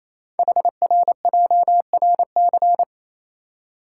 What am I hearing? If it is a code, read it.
Morse code, heard as a keyed tone, about 29 wpm, 700 Hz. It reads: HRJRC
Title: Code Groups - 5 Letters - 28wpm